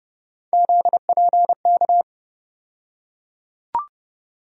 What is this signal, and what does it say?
Morse code, heard as a keyed tone, about 30 wpm, 700 Hz. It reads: ZPK
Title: USA Call Sign Postfixes - 3 Letters 30wpm